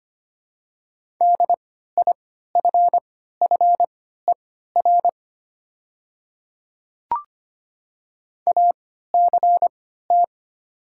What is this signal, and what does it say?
Morse code, heard as a keyed tone, about 25 wpm, spaced as for 14 wpm, 700 Hz. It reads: DIFFER ACT
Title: 3x Character Spacing - Top 200 Words - 25wpm